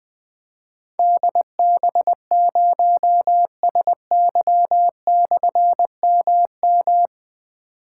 Morse code, heard as a keyed tone, about 20 wpm, 700 Hz. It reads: DB0SY/MM
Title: Difficult Call Signs 20wpm